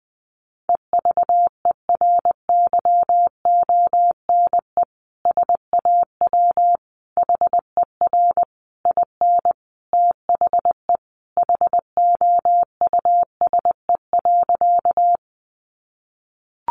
Morse code, heard as a keyed tone, about 20 wpm, 700 Hz. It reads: EVERYONE SAW HER IN THE HOUSE.
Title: Sentences from Top 800 Words 20wpm